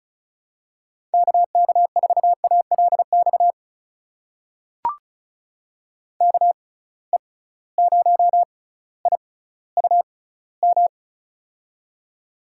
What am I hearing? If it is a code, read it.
Morse code, heard as a keyed tone, about 36 wpm, spaced as for 18 wpm, 700 Hz. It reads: KK4ALX KE0IUM
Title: USA Call Signs - 6 Characters - 6x Character Spacing - 35wpm